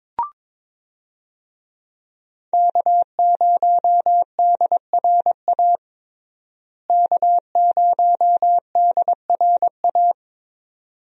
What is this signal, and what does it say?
Morse code, heard as a keyed tone, about 21 wpm, 700 Hz. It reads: K0DRA K0DRA
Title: USA Call Signs - 5 Characters - Repeated 2x - 22wpm